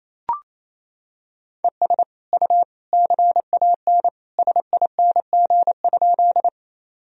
Morse code, heard as a keyed tone, about 28 wpm, 700 Hz. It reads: ES U CAN SING?
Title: Sentences from Top 400 Words and Abbreviations 28wpm